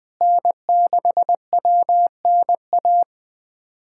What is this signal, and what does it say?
Morse code, heard as a keyed tone, about 20 wpm, 700 Hz. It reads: N6WNA
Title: USA Callsign DE Callsign - 5 Characters - 20wpm